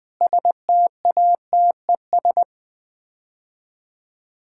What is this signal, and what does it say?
Morse code, heard as a keyed tone, about 21 wpm, 700 Hz. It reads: STATES
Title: Sets of 9 Words - 20wpm